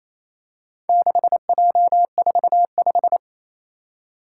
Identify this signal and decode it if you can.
Morse code, heard as a keyed tone, about 28 wpm, 700 Hz. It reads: 6J45
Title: Four Challenging Letter-Number Combinations 28wpm